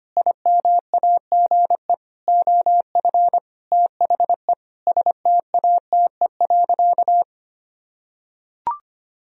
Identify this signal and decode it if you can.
Morse code, heard as a keyed tone, about 25 wpm, 700 Hz. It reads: IMAGE OF THE STATE.
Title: Sentences from Top 1000 Words - Review Part 4 - 25wpm